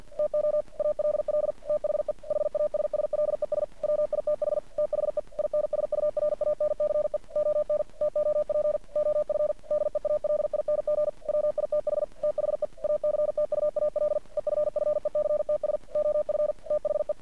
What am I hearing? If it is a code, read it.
Morse code, heard as a keyed tone, about 50 wpm, 600 Hz. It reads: TO ALL THE HASSLES WITH THE INSURANCE ON TOP OF DEALING WITH THE ACTUAL EFFECTS OF THE
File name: archived / CRSnet